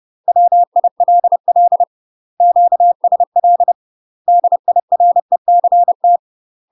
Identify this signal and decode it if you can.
Morse code, heard as a keyed tone, about 30 wpm, 700 Hz. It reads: WILL QSL DIRECT
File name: UK MASTER QSOs TRAINING/30WPM